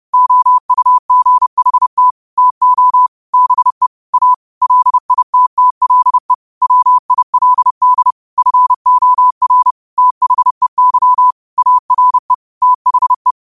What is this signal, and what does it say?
Morse code, heard as a keyed tone, about 30 wpm, 1000 Hz. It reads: OUGHT TO BE A LITTLE WILD FOR THEY ARE THE
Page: Quotes for Sun, 17 Aug 2025 in Morse Code at 30 words per minute.